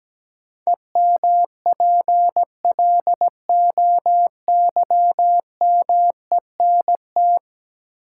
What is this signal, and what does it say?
Morse code, heard as a keyed tone, about 17 wpm, 700 Hz. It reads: EMPLOYMENT
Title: Sentences from Top 1500 Words - New Word Drill - 17wpm